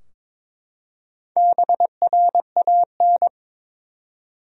Morse code, heard as a keyed tone, about 22 wpm, 700 Hz. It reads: BRAN
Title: USA Common Names - Max 4 Letters - 22wpm